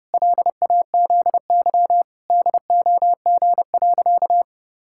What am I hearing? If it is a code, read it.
Morse code, heard as a keyed tone, about 30 wpm, 700 Hz. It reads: LAZY DOG.
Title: Pangram Challenge v2 - 30wpm